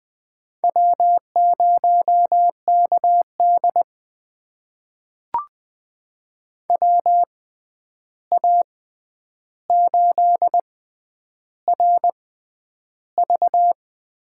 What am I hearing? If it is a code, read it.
Morse code, heard as a keyed tone, about 20 wpm, spaced as for 10 wpm, 700 Hz. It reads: W0KD WA8RV
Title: USA Call Signs - 6x Character Spacing - 20wpm